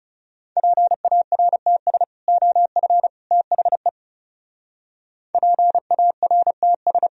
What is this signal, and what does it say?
Morse code, heard as a keyed tone, about 38 wpm, 700 Hz. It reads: PARTS OF THE PARTS